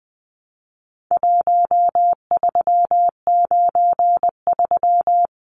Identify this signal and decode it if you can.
Morse code, heard as a keyed tone, about 20 wpm, 700 Hz. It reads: 1393
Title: Numbers - 4-Digit 20wpm